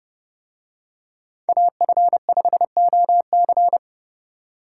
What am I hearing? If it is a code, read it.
Morse code, heard as a keyed tone, about 28 wpm, 700 Hz. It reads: AF5OC